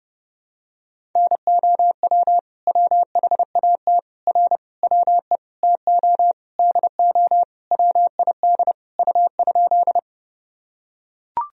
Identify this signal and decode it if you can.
Morse code, heard as a keyed tone, about 32 wpm, 700 Hz. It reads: NOW WHAT R WE TO DO WID U?
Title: Sentences from Top 100 Words and Abbreviations 30wpm